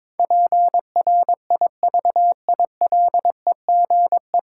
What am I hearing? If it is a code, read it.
Morse code, heard as a keyed tone, about 22 wpm, 700 Hz. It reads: PRIVILEGE
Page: Morse Code Ninja practice 22wpm